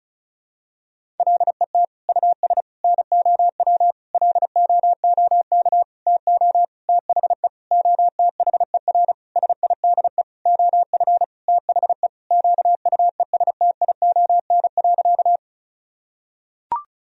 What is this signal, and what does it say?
Morse code, heard as a keyed tone, about 34 wpm, 700 Hz. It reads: LET US NOW LOOK TO THE OTHER SIDE OF THE QUESTION.
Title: Sentences from Top 1000 Words - Review Part 1 - 35wpm